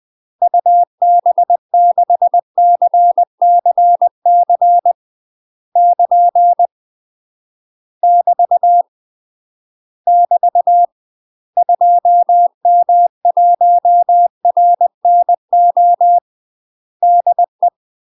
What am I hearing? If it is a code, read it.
Morse code, heard as a keyed tone, about 20 wpm, 700 Hz. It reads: UB6CCC <KN> <BT> <BT> 2M1RNO DE
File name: UK MASTER QSOs TRAINING / 20WPM / UK MASTER QSOs 0055